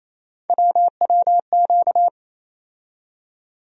Morse code, heard as a keyed tone, about 28 wpm, 700 Hz. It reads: WWQ
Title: USA Call Sign Postfixes - 3 Letters 28wpm